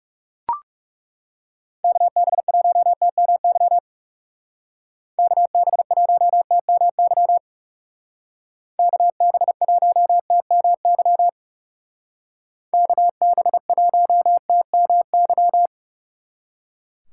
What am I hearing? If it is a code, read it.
Morse code, heard as a keyed tone, about 40 wpm, 700 Hz. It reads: KB1TMY KB1TMY KB1TMY KB1TMY